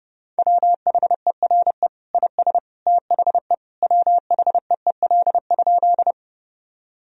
Morse code, heard as a keyed tone, about 32 wpm, 700 Hz. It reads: WHERE IS THE WHEEL?
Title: Sentences from Top 1000 Words - Review Part 3 - 30wpm